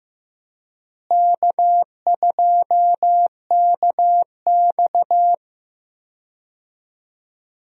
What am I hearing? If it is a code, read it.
Morse code, heard as a keyed tone, about 16 wpm, 700 Hz. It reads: K2KX
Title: USA Call Signs - 4 Characters - Repeated 3x - 15wpm